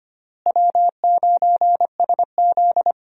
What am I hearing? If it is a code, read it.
Morse code, heard as a keyed tone, about 25 wpm, 700 Hz. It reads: W9SZ
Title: USA Call Signs - 4 Characters - Repeated 3x - 25wpm